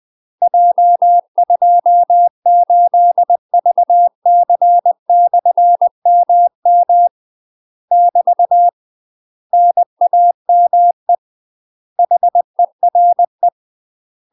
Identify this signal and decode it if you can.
Morse code, heard as a keyed tone, about 20 wpm, 700 Hz. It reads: J28VC/MM <BT> NAME HERE
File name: UK MASTER QSOs TRAINING / 20WPM / UK MASTER QSOs 0032